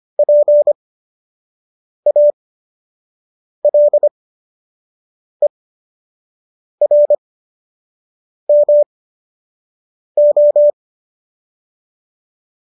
Morse code, heard as a keyed tone, about 25 wpm, spaced as for 6 wpm, 590 Hz. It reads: PALERMO